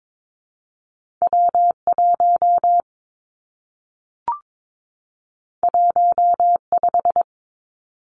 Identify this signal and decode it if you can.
Morse code, heard as a keyed tone, about 22 wpm, 700 Hz. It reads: W 1 1 5
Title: Two Challenging Letter-Number Combinations 22wpm